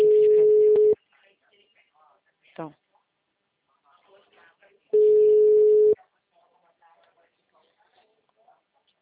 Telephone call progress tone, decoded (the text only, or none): ringback tone